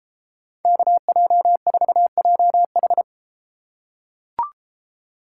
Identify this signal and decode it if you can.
Morse code, heard as a keyed tone, about 32 wpm, 700 Hz. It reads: KJ4JH